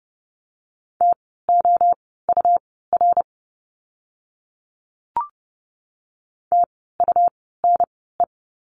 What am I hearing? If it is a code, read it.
Morse code, heard as a keyed tone, about 30 wpm, spaced as for 17 wpm, 700 Hz. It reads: TOUR TUNE
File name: Lesson 17f - Letter u - Words 30wpm